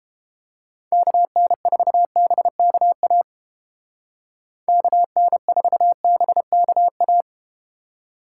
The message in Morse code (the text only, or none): KN4BKA KN4BKA